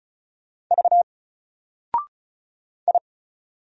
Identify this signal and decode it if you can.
Morse code, heard as a keyed tone, about 36 wpm, 700 Hz. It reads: V I
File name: Single-Letter-Number-35wpm